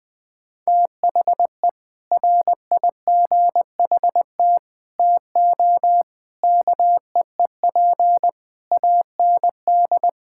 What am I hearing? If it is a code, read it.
Morse code, heard as a keyed tone, about 20 wpm, 700 Hz. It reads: THE RIGHT TO KEEP AND
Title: Sentences with 7-Word Phrases - 20wpm